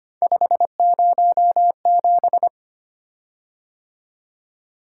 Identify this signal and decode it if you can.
Morse code, heard as a keyed tone, about 25 wpm, 700 Hz. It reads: 507
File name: Three Letter-Number Combinations - 25wpm